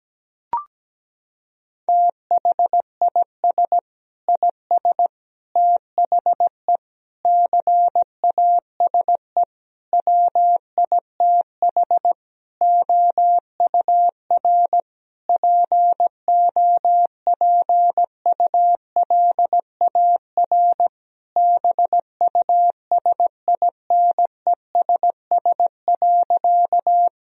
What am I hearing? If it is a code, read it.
Morse code, heard as a keyed tone, about 17 wpm, 700 Hz. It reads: THIS IS THE CASE WITH OUR POPULAR BUSINESS.
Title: Sentences from Top 1000 Words - Review Part 3 - 17wpm